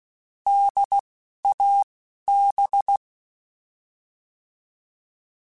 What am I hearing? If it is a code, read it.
Morse code, heard as a keyed tone, about 16 wpm, 800 Hz. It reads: D A B
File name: Alphabetical / Alphabetical mp3s / Ds @10